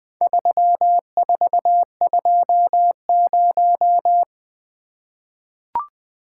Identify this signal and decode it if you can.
Morse code, heard as a keyed tone, about 20 wpm, 700 Hz. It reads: 3420
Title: Four Letter-Number Combinations - 20wpm